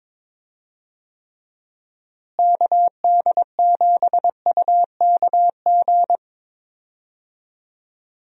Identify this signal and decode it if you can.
Morse code, heard as a keyed tone, about 22 wpm, 700 Hz. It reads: KD7UKG
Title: USA Call Signs - 6 Characters - Repeated 2x - 22wpm